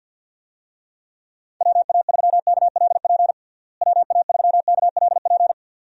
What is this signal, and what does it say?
Morse code, heard as a keyed tone, about 51 wpm, 700 Hz. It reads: WA3KLP WA3KLP
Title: USA Call Signs - 6 Characters - Repeated 2x - 50wpm